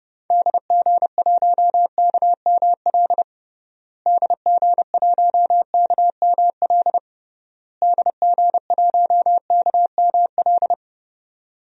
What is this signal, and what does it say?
Morse code, heard as a keyed tone, about 30 wpm, 700 Hz. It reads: DG1KML DG1KML DG1KML
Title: DXCC-100 Call Signs - 6 Characters - Repeated 3x - 30wpm